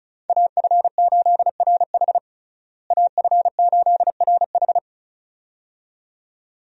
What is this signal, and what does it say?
Morse code, heard as a keyed tone, about 34 wpm, 700 Hz. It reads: AF8RH AF8RH
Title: USA Call Signs - 5 Characters - Repeated 2x - 35wpm